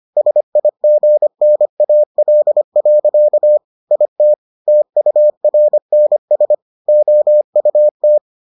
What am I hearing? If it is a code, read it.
Morse code, heard as a keyed tone, about 25 wpm, 600 Hz. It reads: SIGNAL. IT TURNS OUT